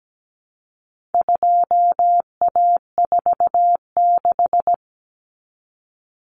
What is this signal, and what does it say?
Morse code, heard as a keyed tone, about 17 wpm, 700 Hz. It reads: 2A46
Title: Common Cut Numbers - 4-Digit 17wpm